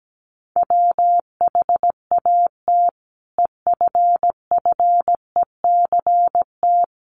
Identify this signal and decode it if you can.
Morse code, heard as a keyed tone, about 17 wpm, 700 Hz. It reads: WHAT EFFECT
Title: Sentences from Top 1000 Words 17wpm